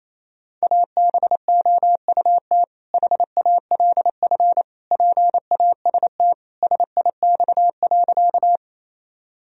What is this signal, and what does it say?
Morse code, heard as a keyed tone, about 27 wpm, 700 Hz. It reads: ABOUT HALF PAST SIX.
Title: Sentences from Top 1000 Words - Review Part 3 - 28wpm